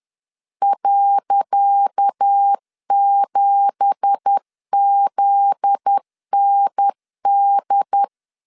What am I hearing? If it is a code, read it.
Morse code, heard as a keyed tone, about 10 wpm, 800 Hz. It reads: .7ZND
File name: LEZIONI / 10 SIGNS / 10C DOT 10wpm 2940